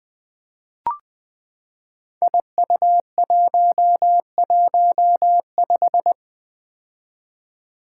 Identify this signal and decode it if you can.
Morse code, heard as a keyed tone, about 20 wpm, 700 Hz. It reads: IU115
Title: Five Challenging Letter-Number Combinations 20wpm